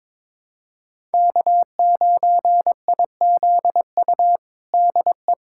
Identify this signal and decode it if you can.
Morse code, heard as a keyed tone, about 22 wpm, 700 Hz. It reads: K9IZU DE
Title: USA Callsign DE Callsign - 5 Characters - 22wpm